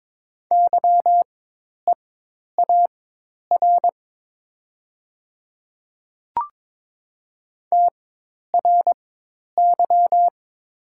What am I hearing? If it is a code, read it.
Morse code, heard as a keyed tone, about 22 wpm, spaced as for 10 wpm, 700 Hz. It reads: YEAR TRY
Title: 4x Character Spacing - Top 500 Words - 22wpm